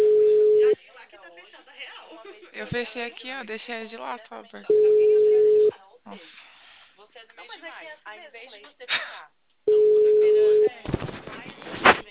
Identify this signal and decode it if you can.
Ringback tone (Europe, 425 Hz), with speech.
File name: out- -2046- - - .32369